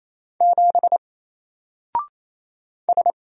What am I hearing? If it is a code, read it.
Morse code, heard as a keyed tone, about 28 wpm, 700 Hz. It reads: 7 S
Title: Single Character (Letter:Number:Punctuation:Prosign) Alt 28wpm